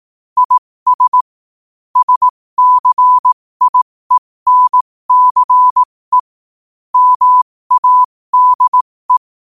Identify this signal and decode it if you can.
Morse code, heard as a keyed tone, about 18 wpm, spaced as for 15 wpm, 1000 Hz. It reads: IS SCIENCE MADE
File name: qotdfiles / 15wpm